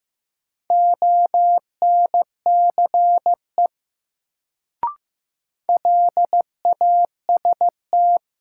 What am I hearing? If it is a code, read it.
Morse code, heard as a keyed tone, about 15 wpm, 700 Hz. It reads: ONCE LAST
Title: Sentences from Top 300 Words - New Word Drill - 15wpm